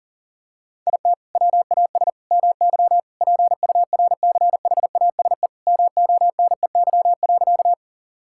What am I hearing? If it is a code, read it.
Morse code, heard as a keyed tone, about 40 wpm, 700 Hz. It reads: IT WAS MY PURCHASE MONEY.
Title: Sentences from Top 1000 Words - Review Part 5 - 40wpm